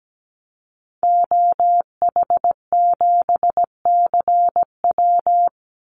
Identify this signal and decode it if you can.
Morse code, heard as a keyed tone, about 17 wpm, 700 Hz. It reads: OH7CW
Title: Call Signs Repeated 3x - 17wpm